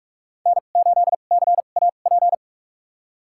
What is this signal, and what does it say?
Morse code, heard as a keyed tone, about 46 wpm, spaced as for 32 wpm, 700 Hz. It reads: N8CAP
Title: USA Call Signs - 5 Characters - 2x Character Spacing - 45wpm